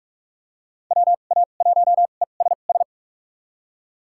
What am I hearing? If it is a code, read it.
Morse code, heard as a keyed tone, about 47 wpm, spaced as for 33 wpm, 700 Hz. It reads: WA1ESS
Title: USA Call Signs - 2x Character Spacing - 45wpm